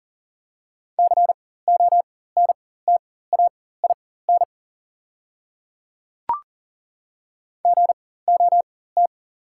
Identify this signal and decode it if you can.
Morse code, heard as a keyed tone, about 40 wpm, spaced as for 19 wpm, 700 Hz. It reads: CONTAIN GOT
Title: 4x Character Spacing - Top 500 Words - 40wpm